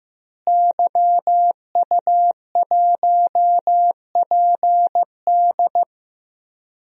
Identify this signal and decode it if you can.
Morse code, heard as a keyed tone, about 15 wpm, 700 Hz. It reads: YU1PD
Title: DXCC-100 Call Signs - 5 Characters - Repeated 2x - 15wpm